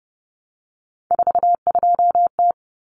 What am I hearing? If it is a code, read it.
Morse code, heard as a keyed tone, about 30 wpm, 700 Hz. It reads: 42T
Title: Common Cut Numbers - 3-Digit 30wpm